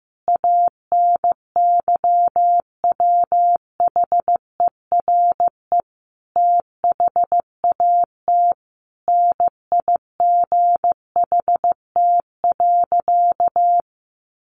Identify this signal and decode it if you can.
Morse code, heard as a keyed tone, about 15 wpm, 700 Hz. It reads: ANYWHERE THAT NIGHT.
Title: Sentences from Top 1400 Words 15wpm